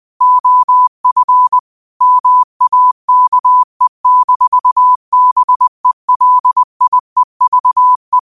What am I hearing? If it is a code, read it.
Morse code, heard as a keyed tone, about 20 wpm, 1000 Hz. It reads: OF MAKE-BELIEVE
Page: Quotes for Fri, 15 Aug 2025 in Morse Code at 20 words per minute.